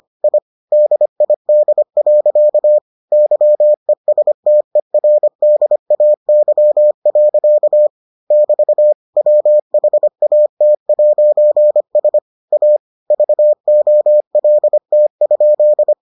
Morse code, heard as a keyed tone, about 25 wpm, 600 Hz. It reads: I DID. YESTERDAY. <BT> WHAT'S A VOLT?